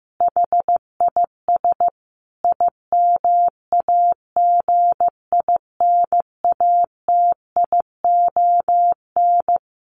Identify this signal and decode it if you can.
Morse code, heard as a keyed tone, about 15 wpm, 700 Hz. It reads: HIS IMAGINATION
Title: Sentences from Top 1300 Words 15wpm